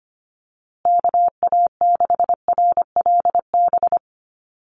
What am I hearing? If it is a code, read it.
Morse code, heard as a keyed tone, about 25 wpm, 700 Hz. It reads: KA6RLB
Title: USA Call Signs Repeated 3x 25wpm